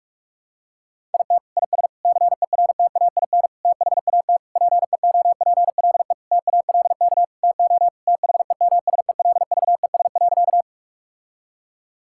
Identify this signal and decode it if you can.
Morse code, heard as a keyed tone, about 41 wpm, 700 Hz. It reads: IT IS CERTAIN THAT PEOPLE TALK TO THEMSELVES.
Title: Sentences from Top 1000 Words - Review Part 5 - 45wpm